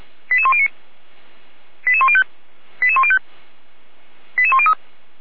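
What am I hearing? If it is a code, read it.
Five-tone selective call (ZVEI), heard as quick runs of five tones, tones of about 70 ms. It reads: 88109 88105 88105 88103
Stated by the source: ZVEI five-tone selective call received on an SDR, fire brigade.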